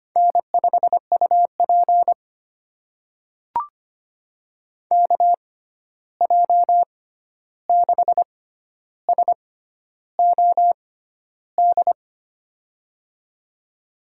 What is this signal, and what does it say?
Morse code, heard as a keyed tone, about 25 wpm, spaced as for 11 wpm, 700 Hz. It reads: N5UP KJ6SOD
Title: USA Call Signs - 6x Character Spacing - 25wpm